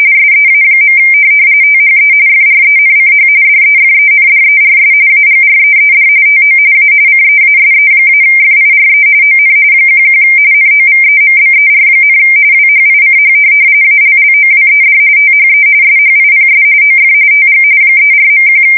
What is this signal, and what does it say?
RTTY, about 45.45 baud. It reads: SEA. THE FRENCH DEFENDERS AGREED TO A CEASE-FIRE ON 11 NOVEMBER, WHICH ALLOWED THE SHIPS OF TG 34.1
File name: fsk170 / USS-Massachusetts